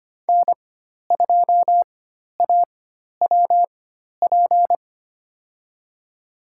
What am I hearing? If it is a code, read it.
Morse code, heard as a keyed tone, about 25 wpm, spaced as for 12 wpm, 700 Hz. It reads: N2AWP
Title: USA Call Signs - 5 Characters - 4x Character Spacing - 25wpm